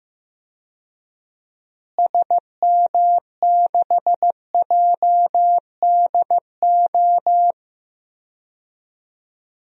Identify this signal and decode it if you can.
Morse code, heard as a keyed tone, about 15 wpm, 700 Hz. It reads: SM6JDO
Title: DXCC-100 Call Signs - 6 Characters - Repeated 3x - 15wpm